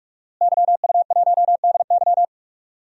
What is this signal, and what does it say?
Morse code, heard as a keyed tone, about 44 wpm, 700 Hz. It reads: YU1DY
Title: DXCC-100 Call Signs - 5 Characters - Repeated 3x - 45wpm